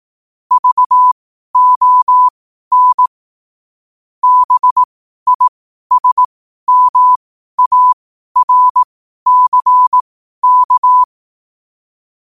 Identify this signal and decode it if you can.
Morse code, heard as a keyed tone, about 18 wpm, spaced as for 12 wpm, 1000 Hz. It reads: VON BISMARCK
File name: qotdfiles / 12wpm